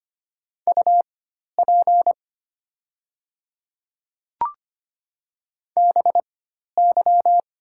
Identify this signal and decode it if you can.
Morse code, heard as a keyed tone, about 25 wpm, spaced as for 12 wpm, 700 Hz. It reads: UP BY
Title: 4x Character Spacing - Top 100 Words - 25wpm